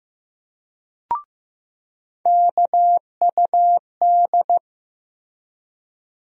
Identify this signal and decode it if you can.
Morse code, heard as a keyed tone, about 15 wpm, 700 Hz. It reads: KUD
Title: USA Call Sign Postfixes - 3 Letters 15wpm